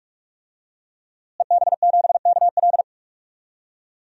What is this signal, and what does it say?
Morse code, heard as a keyed tone, about 46 wpm, 700 Hz. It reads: EB7KL
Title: DXCC-100 Call Signs - 5 Characters - Repeated 2x - 45wpm